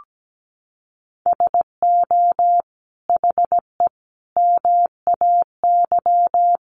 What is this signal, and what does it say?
Morse code, heard as a keyed tone, about 17 wpm, 700 Hz. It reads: SO HE MAY
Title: Sentences from Top 1500 Words 17wpm